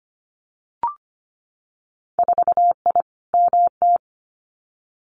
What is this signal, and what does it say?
Morse code, heard as a keyed tone, about 25 wpm, 700 Hz. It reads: 4I MT
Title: Winter Field Day Exchanges Encore 25wpm